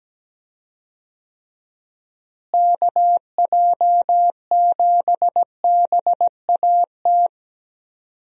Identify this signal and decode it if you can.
Morse code, heard as a keyed tone, about 17 wpm, 700 Hz. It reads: KJ7BAT
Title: USA Call Signs - 6 Characters - Repeated 3x - 17wpm